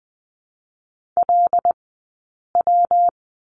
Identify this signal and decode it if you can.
Morse code, heard as a keyed tone, about 20 wpm, 700 Hz. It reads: L W
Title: 120 Lesson 20d - Number 3 - Two Characters 20wpm